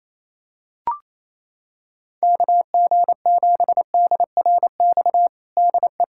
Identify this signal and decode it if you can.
Morse code, heard as a keyed tone, about 28 wpm, 700 Hz. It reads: KG7DRX DE
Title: USA Callsign DE Callsign 28wpm